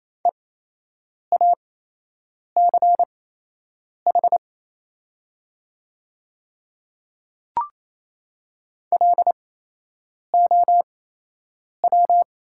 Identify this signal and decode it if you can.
Morse code, heard as a keyed tone, about 28 wpm, spaced as for 8 wpm, 700 Hz. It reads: EACH LOW
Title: 8x Character Spacing - Top 200 Words - 28wpm